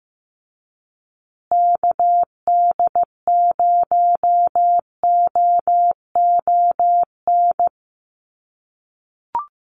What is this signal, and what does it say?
Morse code, heard as a keyed tone, about 15 wpm, 700 Hz. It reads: KD0OON
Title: USA Call Signs Repeated 2x 15wpm